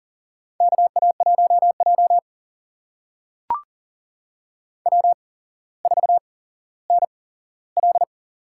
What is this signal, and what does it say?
Morse code, heard as a keyed tone, about 40 wpm, spaced as for 17 wpm, 700 Hz. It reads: KA1J W4NL
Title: USA Call Signs - 4 Characters - 8x Character Spacing - 40wpm